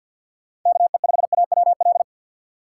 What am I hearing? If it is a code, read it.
Morse code, heard as a keyed tone, about 51 wpm, 700 Hz. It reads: KE5AWL
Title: USA Call Signs - 6 Characters - 3x Character Spacing - 50wpm